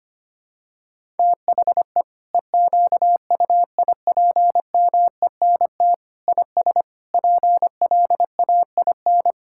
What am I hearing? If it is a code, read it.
Morse code, heard as a keyed tone, about 25 wpm, 700 Hz. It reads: THE EQUIPMENT IS PLAIN
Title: Sentences from Top 1000 Words - Review Part 5 - 25wpm